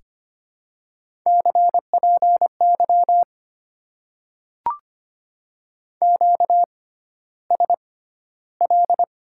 Morse code, heard as a keyed tone, about 25 wpm, spaced as for 13 wpm, 700 Hz. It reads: CPY QSL
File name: Top 90 QSO Elements - 6x Character Spacing - 25wpm